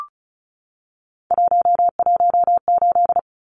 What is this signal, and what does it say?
Morse code, heard as a keyed tone, about 34 wpm, 700 Hz. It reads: 118
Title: Numbers - 3-Digit 35wpm